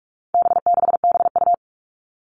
Morse code, heard as a keyed tone, about 44 wpm, 700 Hz. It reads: B6BU
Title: All Cut Numbers - 4-Digit 45wpm